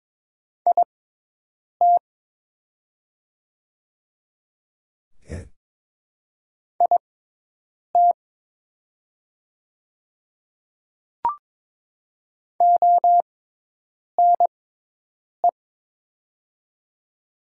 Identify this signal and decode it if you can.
Morse code, heard as a keyed tone, about 22 wpm, spaced as for 8 wpm, 700 Hz. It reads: IT IT ONE
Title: 6x Character Spacing - Top 200 Words - 22wpm